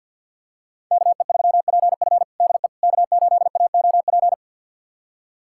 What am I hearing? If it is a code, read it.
Morse code, heard as a keyed tone, about 53 wpm, 690 Hz. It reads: KE3PF DE K8AOP
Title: USA Callsign DE Callsign - 5 Characters - 50wpm